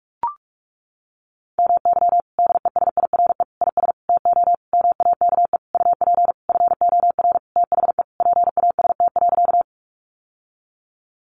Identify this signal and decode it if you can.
Morse code, heard as a keyed tone, about 44 wpm, 700 Hz. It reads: MY DESIRE IS TO MAKE UP FOR THE PAST.
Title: Sentences from Top 1100 Words 45wpm